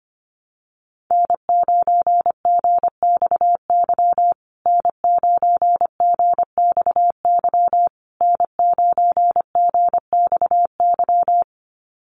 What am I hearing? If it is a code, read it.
Morse code, heard as a keyed tone, about 25 wpm, 700 Hz. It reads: N9GXY N9GXY N9GXY
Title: USA Call Signs Repeated 3x 25wpm